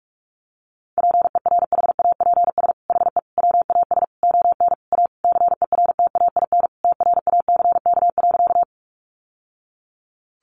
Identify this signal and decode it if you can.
Morse code, heard as a keyed tone, about 46 wpm, 700 Hz. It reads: PERHAPS HE WAS ON A CERTAIN TRACK.
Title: Sentences from Top 1100 Words 45wpm